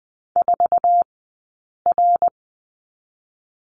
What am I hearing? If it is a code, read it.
Morse code, heard as a keyed tone, about 20 wpm, 700 Hz. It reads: 4 R
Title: 052 Lesson 10d - Letter r - Two Characters 20wpm